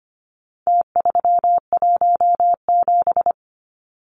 Morse code, heard as a keyed tone, about 25 wpm, 700 Hz. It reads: T317
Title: Numbers - Contest Style - 4-Digit 25wpm